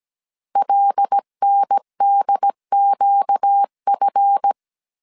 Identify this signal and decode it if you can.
Morse code, heard as a keyed tone, about 17 wpm, 800 Hz. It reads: LNDQF
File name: CW / DET / 1 9 1500C S C 16wpm 2909